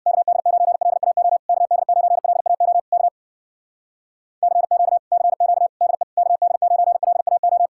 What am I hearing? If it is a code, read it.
Morse code, heard as a keyed tone, about 63 wpm, 700 Hz. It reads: KD9LAO KD9LAO K CQ CQ DE KD9LAO